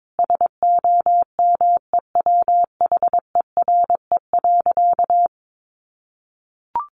Morse code, heard as a keyed tone, about 22 wpm, 700 Hz. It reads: SOMEWHERE.
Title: Sentences from Top 1400 Words 22wpm